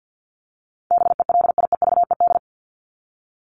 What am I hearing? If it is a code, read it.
Morse code, heard as a keyed tone, about 50 wpm, 700 Hz. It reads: BELIEVED
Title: ED Suffix - Top 100 Words with Base Words 50wpm